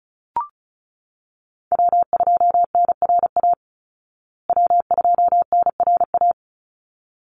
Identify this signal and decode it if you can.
Morse code, heard as a keyed tone, about 34 wpm, 700 Hz. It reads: W2NRA W2NRA
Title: Call Signs Repeated 3x - 35wpm